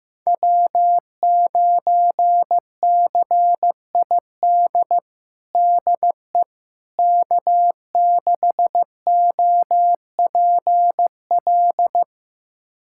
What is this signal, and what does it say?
Morse code, heard as a keyed tone, about 15 wpm, 700 Hz. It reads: W9CID DE K6OPL
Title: USA Callsign DE Callsign - 5 Characters - 15wpm